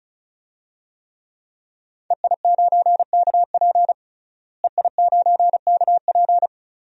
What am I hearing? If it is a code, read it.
Morse code, heard as a keyed tone, about 36 wpm, 700 Hz. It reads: EI9KP EI9KP
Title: DXCC-100 Call Signs - 5 Characters - Repeated 2x - 35wpm